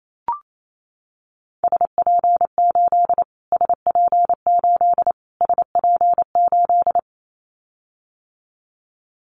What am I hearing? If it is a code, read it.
Morse code, heard as a keyed tone, about 28 wpm, 700 Hz. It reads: SP8 SP8 SP8
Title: Callsign Prefixes for Top 25 DXCC - Repeated 3x - 28wpm